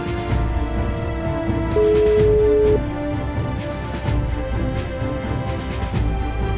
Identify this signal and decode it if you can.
Ringback tone (Europe, 425 Hz), with noise.